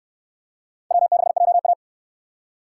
Morse code, heard as a keyed tone, about 68 wpm, 700 Hz. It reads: W6JA